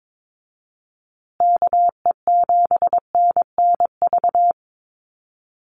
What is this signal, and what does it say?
Morse code, heard as a keyed tone, about 22 wpm, 700 Hz. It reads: KE7NNV
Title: USA Call Signs Repeated 2x 22wpm